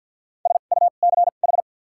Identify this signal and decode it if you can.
Morse code, heard as a keyed tone, about 53 wpm, 700 Hz. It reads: S U C H
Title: 2x Character Spacing - Top 200 Words - 50wpm